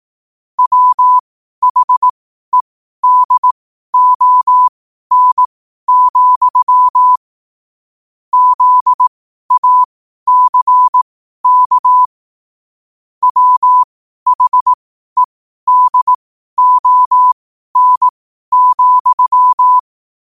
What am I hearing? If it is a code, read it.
Morse code, heard as a keyed tone, about 18 wpm, spaced as for 12 wpm, 1000 Hz. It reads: WHEDON, ZACK WHEDON,
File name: qotdfiles / 12wpm